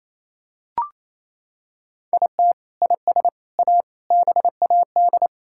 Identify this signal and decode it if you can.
Morse code, heard as a keyed tone, about 28 wpm, 700 Hz. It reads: IT IS A BAD
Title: Sentences from Top 1000 Words - Review Part 6 - 28wpm